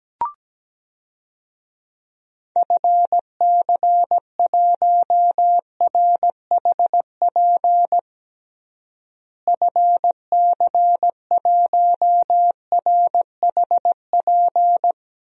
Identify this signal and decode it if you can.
Morse code, heard as a keyed tone, about 17 wpm, 700 Hz. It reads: FC1RHP FC1RHP
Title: DXCC-100 Call Signs - 6 Characters - Repeated 3x - 17wpm